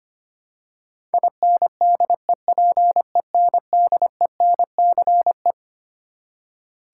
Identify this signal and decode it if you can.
Morse code, heard as a keyed tone, about 25 wpm, 700 Hz. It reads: INDEPENDENCE
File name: Twelve Letter Words - 25wpm